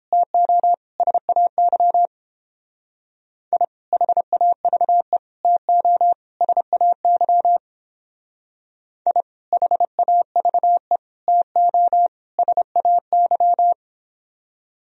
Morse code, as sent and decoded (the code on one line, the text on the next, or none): - --- / ... .- -.-- / .. / .... .- ...- . / - --- / ... .- -.-- / .. / .... .- ...- . / - --- / ... .- -.--
TO SAY I HAVE TO SAY I HAVE TO SAY